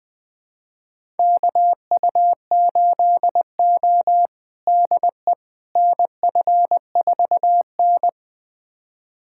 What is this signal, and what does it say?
Morse code, heard as a keyed tone, about 20 wpm, 700 Hz. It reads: KU8O DE NF4N
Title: USA Callsign DE Callsign - 4 Characters - 20wpm